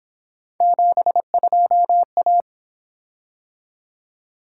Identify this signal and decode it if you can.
Morse code, heard as a keyed tone, about 27 wpm, 700 Hz. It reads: 72A
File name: Speed-Racing - Three Letter-Number Combinations - 22wpm